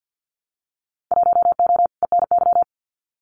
Morse code, heard as a keyed tone, about 52 wpm, 700 Hz. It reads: 1O ENY
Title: Winter Field Day Exchanges 50wpm